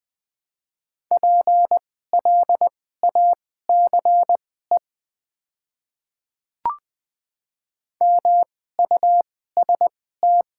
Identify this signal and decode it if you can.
Morse code, heard as a keyed tone, about 20 wpm, spaced as for 14 wpm, 700 Hz. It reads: PLACE MUST
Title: 2x Character Spacing - Top 200 Words - 20wpm